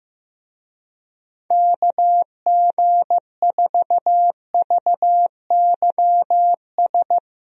Morse code, heard as a keyed tone, about 15 wpm, 700 Hz. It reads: KG4VYS
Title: USA Call Signs - 6 Characters - Repeated 2x - 15wpm